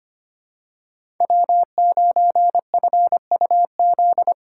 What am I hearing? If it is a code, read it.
Morse code, heard as a keyed tone, about 25 wpm, 700 Hz. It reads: W9FUZ
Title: USA Call Signs - 6x Character Spacing - 25wpm